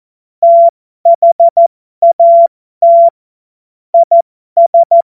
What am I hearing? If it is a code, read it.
Morse code, heard as a keyed tone, about 14 wpm, spaced as for 12 wpm, 680 Hz. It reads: THAT IS
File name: Morsekurs / Witze